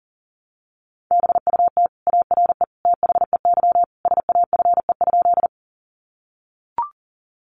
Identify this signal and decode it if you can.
Morse code, heard as a keyed tone, about 37 wpm, 700 Hz. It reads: BUT ARE THEY SAFE?
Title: Sentences from Top 900 Words 40wpm